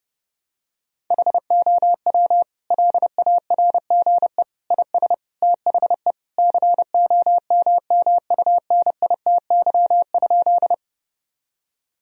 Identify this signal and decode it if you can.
Morse code, heard as a keyed tone, about 31 wpm, 700 Hz. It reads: HOW LARGE IS THE COMMUNITY?
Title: Sentences from Top 1000 Words - Review Part 4 - 30wpm